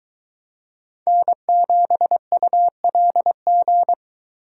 Morse code, heard as a keyed tone, about 22 wpm, 700 Hz. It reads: N7ULG